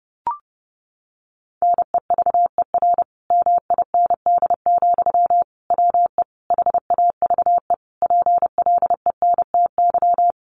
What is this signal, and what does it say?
Morse code, heard as a keyed tone, about 30 wpm, 700 Hz. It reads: NEVER MIND, WE HAVE PLENTY
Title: Sentences from Top 1300 Words 30wpm